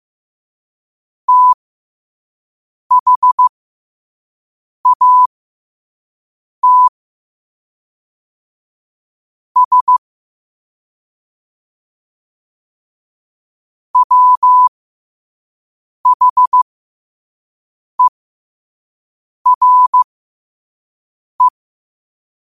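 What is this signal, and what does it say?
Morse code, heard as a keyed tone, about 15 wpm, spaced as for 5 wpm, 1000 Hz. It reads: THAT S WHERE
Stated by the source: Morse code sent at 5 wpm.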